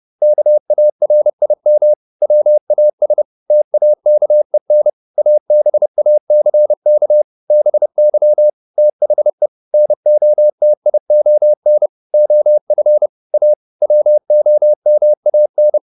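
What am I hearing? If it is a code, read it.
Morse code, heard as a keyed tone, about 30 wpm, 600 Hz. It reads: KARIM WAS TAKEN ABACK BY THE NOTION OF A WOMAN
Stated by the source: Morse code at 30 wpm, the novel Never by Ken Follett in English, from CW med Gnister.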